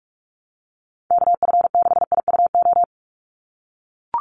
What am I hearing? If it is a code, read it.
Morse code, heard as a keyed tone, about 44 wpm, 700 Hz. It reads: KF6IUO
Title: USA Call Signs 45wpm